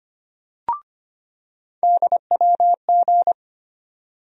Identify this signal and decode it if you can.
Morse code, heard as a keyed tone, about 25 wpm, 700 Hz. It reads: DWG
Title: USA Call Sign Postfixes - 3 Letters 25wpm